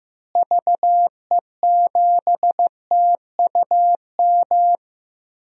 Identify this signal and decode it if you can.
Morse code, heard as a keyed tone, about 15 wpm, 700 Hz. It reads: VE7TUM
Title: DXCC-100 Call Signs - 6 Characters - Repeated 2x - 15wpm